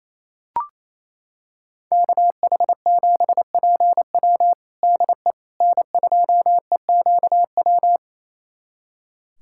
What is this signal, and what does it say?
Morse code, heard as a keyed tone, about 27 wpm, 700 Hz. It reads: KH7PW DE N2EQW
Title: USA Callsign DE Callsign - 5 Characters - 28wpm